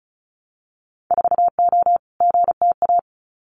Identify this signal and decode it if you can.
Morse code, heard as a keyed tone, about 34 wpm, 700 Hz. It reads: 4O GTA
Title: Winter Field Day Exchanges 35wpm